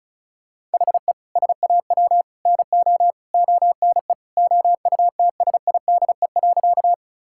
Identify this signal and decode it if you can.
Morse code, heard as a keyed tone, about 36 wpm, 700 Hz. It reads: HE SAW NO ONE OUTSIDE.
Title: Sentences from Top 1000 Words - Review Part 6 - 35wpm